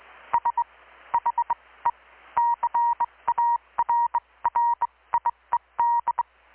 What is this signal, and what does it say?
Morse code, heard as a keyed tone, about 21 wpm, spaced as for 17 wpm, 940 Hz. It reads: SHE CARRIED
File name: blog / swl / 6-abril-2019-12695-USB-KPS